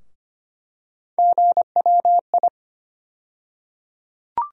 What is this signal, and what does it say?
Morse code, heard as a keyed tone, about 25 wpm, 700 Hz. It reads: GWI
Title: USA Call Sign Postfixes - 3 Letters 25wpm